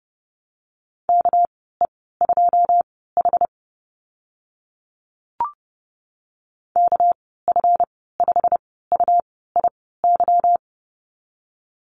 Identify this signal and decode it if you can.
Morse code, heard as a keyed tone, about 30 wpm, spaced as for 17 wpm, 700 Hz. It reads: KE2H KF5UIY
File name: Lesson 37g - Letter x - Call Signs 30wpm